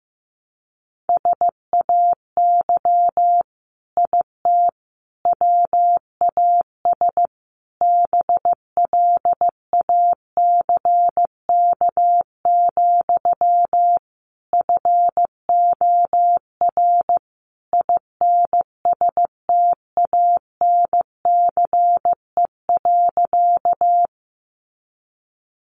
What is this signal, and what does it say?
Morse code, heard as a keyed tone, about 15 wpm, 700 Hz. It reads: SAY IT WAS BLACK, FOR INSTANCE.
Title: Sentences from Top 800 Words 15wpm